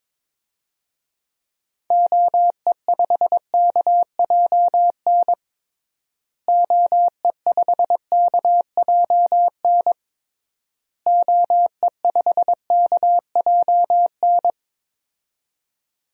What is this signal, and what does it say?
Morse code, heard as a keyed tone, about 21 wpm, 700 Hz. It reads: OE5KJN OE5KJN OE5KJN
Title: DXCC-100 Call Signs - 6 Characters - Repeated 3x - 22wpm